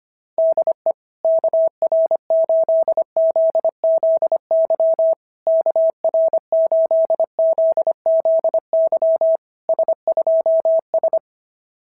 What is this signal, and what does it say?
Morse code, heard as a keyed tone, about 25 wpm, 650 Hz. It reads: DE KR8ZZY KR8ZZY S2S
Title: SOTA QSO Practice 25wpm